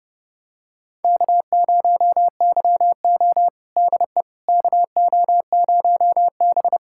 Morse code, heard as a keyed tone, about 31 wpm, 700 Hz. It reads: K0YO DE KO0B
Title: USA Callsign DE Callsign - 4 Characters - 30wpm